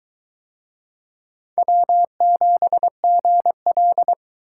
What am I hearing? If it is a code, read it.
Morse code, heard as a keyed tone, about 22 wpm, 700 Hz. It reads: W7GL